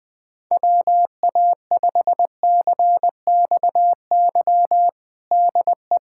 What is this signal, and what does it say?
Morse code, heard as a keyed tone, about 20 wpm, 700 Hz. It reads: WA5CXY DE
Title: USA Callsign DE Callsign 20wpm